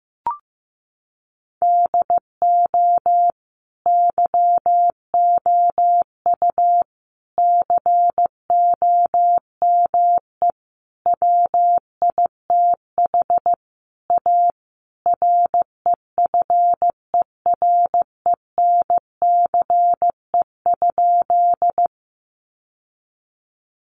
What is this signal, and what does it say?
Morse code, heard as a keyed tone, about 15 wpm, 700 Hz. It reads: DO YOU COME WITH A REFERENCE?
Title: Sentences from Top 1200 Words 15wpm